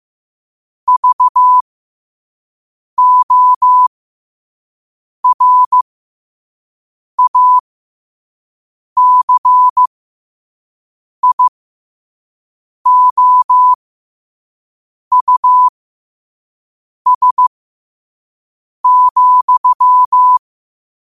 Morse code, heard as a keyed tone, about 15 wpm, spaced as for 5 wpm, 1000 Hz. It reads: VORACIOUS,
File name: qotdfiles / 5wpm